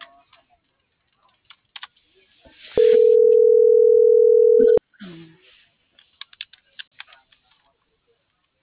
Ringback tone (North America), 440 + 480 Hz.